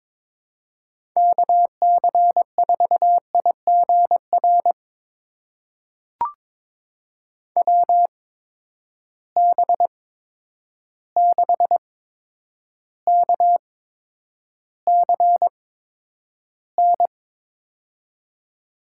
Morse code, heard as a keyed tone, about 22 wpm, spaced as for 9 wpm, 700 Hz. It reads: KC4IGR WB6KCN
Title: USA Call Signs - 6 Characters - 8x Character Spacing - 22wpm